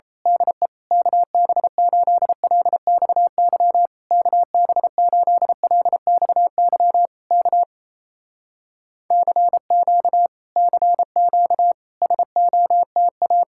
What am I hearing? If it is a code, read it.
Morse code, heard as a keyed tone, about 28 wpm, 700 Hz. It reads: DE KB8LXY KB8LXY K CQ CQ SOTA
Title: Speed-Racing Common QSO Phrases Encore - 28wpm